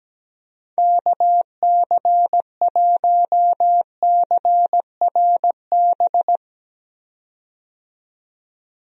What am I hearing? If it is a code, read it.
Morse code, heard as a keyed tone, about 17 wpm, 700 Hz. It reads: KC1CRB
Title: USA Call Signs - 6 Characters - 17wpm